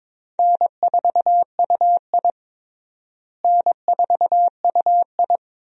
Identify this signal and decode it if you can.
Morse code, heard as a keyed tone, about 22 wpm, 700 Hz. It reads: N4UI N4UI
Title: USA Call Signs - 4 Characters - Repeated 2x - 22wpm